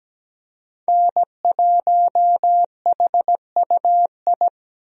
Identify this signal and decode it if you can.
Morse code, heard as a keyed tone, about 17 wpm, 700 Hz. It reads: N1HUI
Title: USA Call Signs - 5 Characters - Repeated 3x - 17wpm